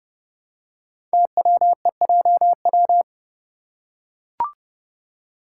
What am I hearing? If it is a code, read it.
Morse code, heard as a keyed tone, about 30 wpm, 690 Hz. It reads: TWEJW
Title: Code Groups - 5 Letters - 30wpm